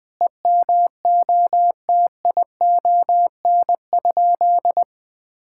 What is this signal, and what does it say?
Morse code, heard as a keyed tone, about 21 wpm, 700 Hz. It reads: EMOTION?
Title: Sentences from Top 1600 Words 20wpm